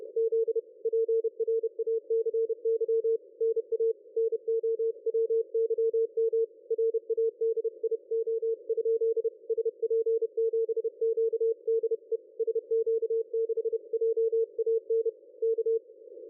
Morse code, heard as a keyed tone, about 31 wpm, 450 Hz. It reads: Z PRACY NA NOWYM RADIO? SP7QDE SQ6JAN K
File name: dlazaawansowanych / nagraniazpasma / ExpertSDR2 line out 23.01.2020 14.22.53